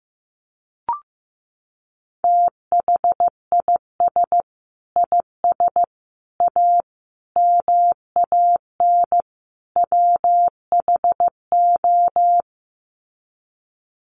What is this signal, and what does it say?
Morse code, heard as a keyed tone, about 15 wpm, 700 Hz. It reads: THIS IS A MAN WHO
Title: Sets of 5 Words Encore2 15wpm